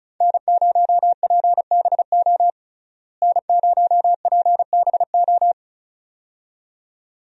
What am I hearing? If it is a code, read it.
Morse code, heard as a keyed tone, about 34 wpm, 700 Hz. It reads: N0PBO N0PBO
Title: USA Call Signs - 5 Characters - Repeated 2x - 35wpm